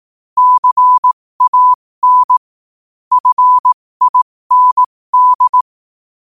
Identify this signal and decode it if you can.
Morse code, heard as a keyed tone, about 18 wpm, spaced as for 15 wpm, 1000 Hz. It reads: CAN FIND